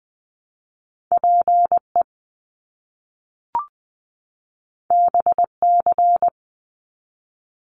Morse code, heard as a keyed tone, about 20 wpm, 700 Hz. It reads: PE BC
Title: Canadian Provinces and Territories 20wpm